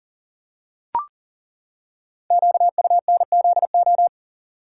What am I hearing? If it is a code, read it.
Morse code, heard as a keyed tone, about 39 wpm, 700 Hz. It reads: QUNZO
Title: Code Groups - 5 Letters - 40wpm